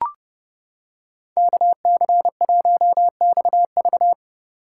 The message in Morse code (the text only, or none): KC1XV